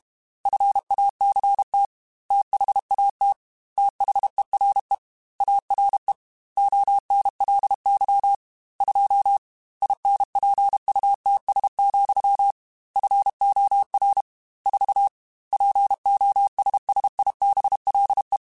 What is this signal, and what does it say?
Morse code, heard as a keyed tone, about 32 wpm, 800 Hz. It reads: FACT THAT THERE ARE ONLY 2 INPUTS, FOR 4 POSSIBLE
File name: Technical Subjects / Technical Subjects-mp3s / AND Gate @30